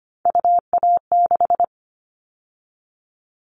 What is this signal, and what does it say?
Morse code, heard as a keyed tone, about 25 wpm, 700 Hz. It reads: UA6
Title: Callsign Prefixes for Top 25 DXCC - 25wpm